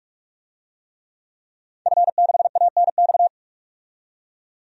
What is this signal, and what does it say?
Morse code, heard as a keyed tone, about 46 wpm, 700 Hz. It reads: F6ANX
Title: DXCC-100 Call Signs - 5 Characters - 45wpm